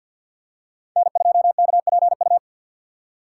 Morse code, heard as a keyed tone, about 52 wpm, 700 Hz. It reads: N2KPU
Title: USA Call Signs - 5 Characters - 6x Character Spacing - 50wpm